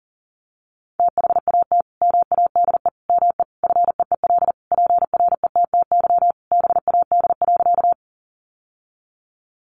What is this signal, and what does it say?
Morse code, heard as a keyed tone, about 40 wpm, 700 Hz. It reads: THAT MADE ME FEEL PRETTY BAD.
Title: Sentences from Top 1000 Words 40wpm